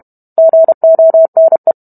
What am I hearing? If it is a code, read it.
Morse code, heard as a keyed tone, about 32 wpm, 650 Hz. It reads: GONE